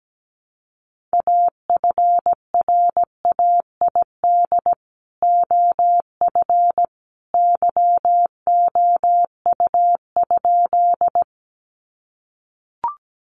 Morse code, heard as a keyed tone, about 17 wpm, 700 Hz. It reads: AFRAID OF YOU?
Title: Sentences from Top 1500 Words 17wpm